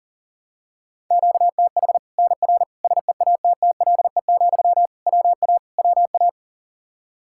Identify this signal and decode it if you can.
Morse code, heard as a keyed tone, about 39 wpm, 700 Hz. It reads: QTH NR SEATTLE, WA WA
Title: Practice QSO - Exchange 1 - Beginning - 40wpm